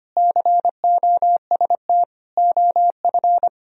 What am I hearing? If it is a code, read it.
Morse code, heard as a keyed tone, about 31 wpm, 700 Hz. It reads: COST OF